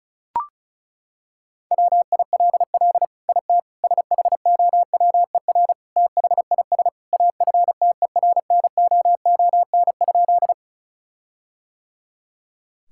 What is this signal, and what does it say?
Morse code, heard as a keyed tone, about 34 wpm, 700 Hz. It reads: WILL IT SHOWER THIS AFTERNOON?
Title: Sentences from Top 1600 Words 35wpm